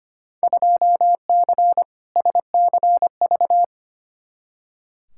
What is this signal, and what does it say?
Morse code, heard as a keyed tone, about 25 wpm, 700 Hz. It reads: 2C SCV
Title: Field Day Exchanges Encore - 25wpm